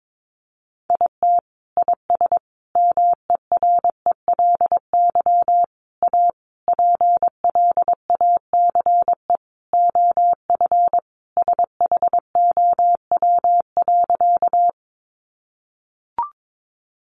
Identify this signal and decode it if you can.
Morse code, heard as a keyed tone, about 22 wpm, 700 Hz. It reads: IT IS MERELY A PLACE OF SHOW.
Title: Sentences from Top 1400 Words 22wpm